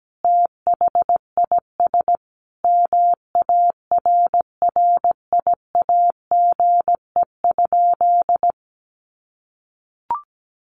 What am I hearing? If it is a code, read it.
Morse code, heard as a keyed tone, about 17 wpm, 700 Hz. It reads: THIS MARRIAGE?
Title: Sentences from Top 1400 Words 17wpm